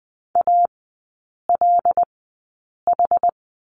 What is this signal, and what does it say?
Morse code, heard as a keyed tone, about 19 wpm, spaced as for 8 wpm, 700 Hz. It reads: ALH
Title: 074 Lesson 13e - Letter L - Three Characters 20wpm